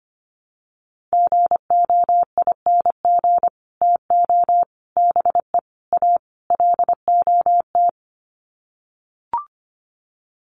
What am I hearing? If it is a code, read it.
Morse code, heard as a keyed tone, about 25 wpm, 700 Hz. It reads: GOING TO BE A LOT
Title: Sets of 5 Words Encore 25wpm